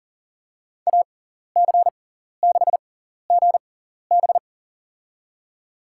Morse code, heard as a keyed tone, about 40 wpm, spaced as for 14 wpm, 700 Hz. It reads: AC6GB
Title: USA Call Signs - 5 Characters - 6x Character Spacing - 40wpm